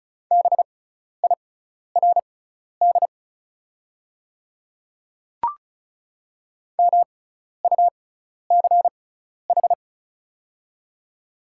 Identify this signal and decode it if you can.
Morse code, heard as a keyed tone, about 36 wpm, spaced as for 12 wpm, 690 Hz. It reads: BIRD MUCH
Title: 6x Character Spacing - Top 500 Words - 35wpm